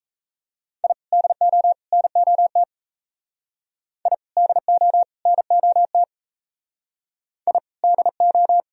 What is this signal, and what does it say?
Morse code, heard as a keyed tone, about 35 wpm, 700 Hz. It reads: I DO NOT I DO NOT I DO
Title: Speed-Racing - Sets of 3 Words 28wpm